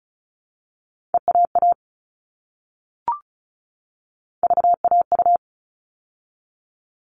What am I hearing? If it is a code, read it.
Morse code, heard as a keyed tone, about 34 wpm, 700 Hz. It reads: EAA VAU
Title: All Cut Numbers - 3-Digit 35wpm